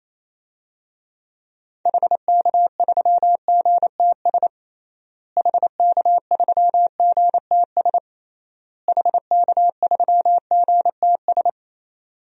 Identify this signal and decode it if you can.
Morse code, heard as a keyed tone, about 28 wpm, 700 Hz. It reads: HK3GTS HK3GTS HK3GTS
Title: DXCC-100 Call Signs - 6 Characters - Repeated 3x - 28wpm